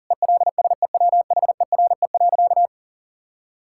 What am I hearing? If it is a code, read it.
Morse code, heard as a keyed tone, about 40 wpm, 700 Hz. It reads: ELSEWHERE.
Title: Sentences from Top 1600 Words 40wpm